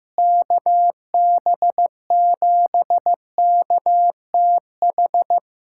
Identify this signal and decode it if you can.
Morse code, heard as a keyed tone, about 15 wpm, 700 Hz. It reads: KB7KTH
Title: USA Callsign DE Callsign 15wpm